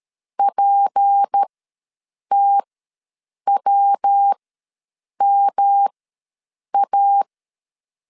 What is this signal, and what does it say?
Morse code, heard as a keyed tone, about 13 wpm, spaced as for 7 wpm, 800 Hz. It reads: PTWMA
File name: LEZIONI / 7 GF9Z / 7 G ws 12wpm 3140